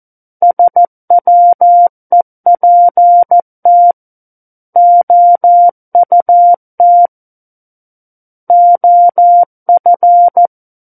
Morse code, heard as a keyed tone, about 14 wpm, 700 Hz. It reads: SWEPT OUT OF
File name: Book One / 05-Chapter 5 - 14 WPM